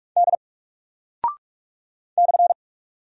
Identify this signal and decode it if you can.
Morse code, heard as a keyed tone, about 44 wpm, 700 Hz. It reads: D /
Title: Single Character (Letter:Number:Punctuation:Prosign) 45wpm